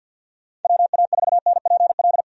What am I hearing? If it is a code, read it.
Morse code, heard as a keyed tone, about 52 wpm, 700 Hz. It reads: WA4NPL
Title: USA Call Signs - 6 Characters - Repeated 3x - 50wpm